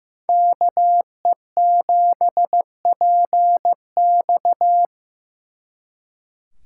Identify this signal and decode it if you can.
Morse code, heard as a keyed tone, about 18 wpm, 700 Hz. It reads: KE7PX
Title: Speed Racing - USA Call Signs 15wpm